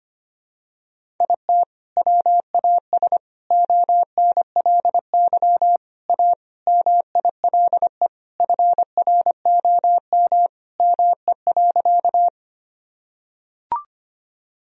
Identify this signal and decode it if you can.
Morse code, heard as a keyed tone, about 25 wpm, 700 Hz. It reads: IT WAS ONLY A MILE FROM ME.
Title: Sentences from Top 1000 Words - Review Part 1 - 25wpm